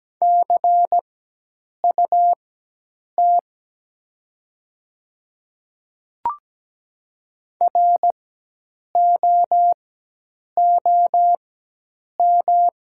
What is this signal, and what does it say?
Morse code, heard as a keyed tone, about 17 wpm, spaced as for 8 wpm, 700 Hz. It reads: CUT ROOM
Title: 4x Character Spacing - Top 300 Words - 17wpm